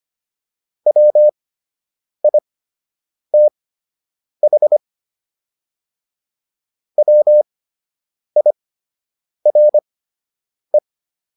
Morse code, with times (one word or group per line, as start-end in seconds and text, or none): 0.9-4.8 WITH
7.0-10.8 WIRE